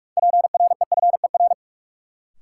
Morse code, heard as a keyed tone, about 42 wpm, 700 Hz. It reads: PREFER
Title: Sentences from Top 1100 Words - New Word Drill - 45wpm